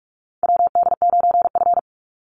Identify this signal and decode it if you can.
Morse code, heard as a keyed tone, about 46 wpm, 690 Hz. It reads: WD9F
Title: NA SOTA Chaser Callsigns 45wpm